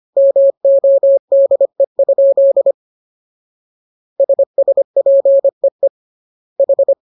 Morse code, heard as a keyed tone, about 25 wpm, 550 Hz. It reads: MODE? SSPEE H